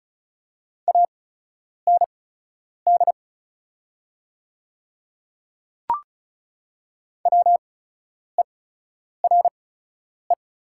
Morse code, heard as a keyed tone, about 35 wpm, spaced as for 10 wpm, 690 Hz. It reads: AND WERE
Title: 8x Character Spacing - Top 100 Words - 35wpm